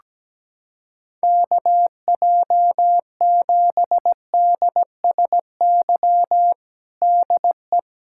Morse code, heard as a keyed tone, about 17 wpm, 700 Hz. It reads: KJ7DSY DE
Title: K1USN SST QSO Practice 17wpm